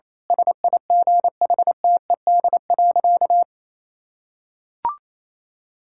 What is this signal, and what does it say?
Morse code, heard as a keyed tone, about 28 wpm, 700 Hz. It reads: SIGHTED.
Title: Pangram Challenge v2 - 28wpm